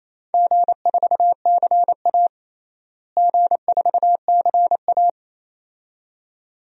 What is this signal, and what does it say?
Morse code, heard as a keyed tone, about 28 wpm, 700 Hz. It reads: G4CA G4CA
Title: DXCC-100 Call Signs - 4 Characters - Repeated 2x - 28wpm